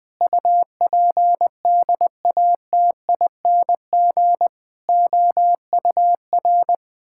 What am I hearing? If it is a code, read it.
Morse code, heard as a keyed tone, about 20 wpm, 700 Hz. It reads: UPDATING OUR
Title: Sentences with 6-Word Phrases - Spoken Beginning - 20wpm